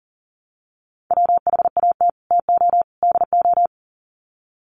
Morse code, heard as a keyed tone, about 40 wpm, 700 Hz. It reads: WHAT TO DO
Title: Sets of 3 Words 40wpm